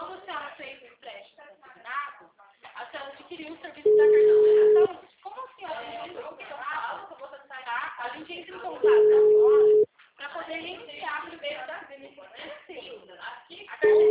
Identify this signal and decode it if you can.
Ringback tone (Europe, 425 Hz), with speech.
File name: out- -2025- - - .49716